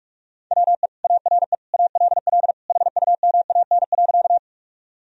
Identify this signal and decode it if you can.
Morse code, heard as a keyed tone, about 47 wpm, 700 Hz. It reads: WE ARE ALL HUMAN.
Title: Sentences from Top 1000 Words - Review Part 3 - 45wpm